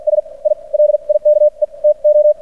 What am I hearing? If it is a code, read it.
Morse code, heard as a keyed tone, about 56 wpm, spaced as for 42 wpm, 600 Hz. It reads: S I LIKE TO
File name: archived / CRSnet